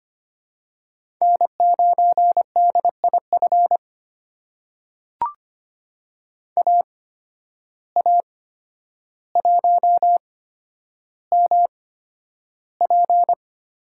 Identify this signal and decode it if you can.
Morse code, heard as a keyed tone, about 25 wpm, spaced as for 11 wpm, 700 Hz. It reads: N9DIF AA1MP
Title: USA Call Signs - 5 Characters - 8x Character Spacing - 25wpm